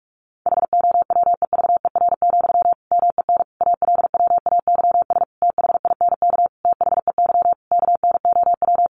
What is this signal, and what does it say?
Morse code, heard as a keyed tone, about 44 wpm, 700 Hz. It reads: HOWEVER, MEN ALWAYS THINK THEY KNOW
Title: Sentences from Top 600 Words 45wpm